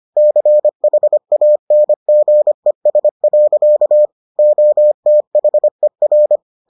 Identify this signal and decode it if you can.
Morse code, heard as a keyed tone, about 25 wpm, 600 Hz. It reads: CHANGES. OTHER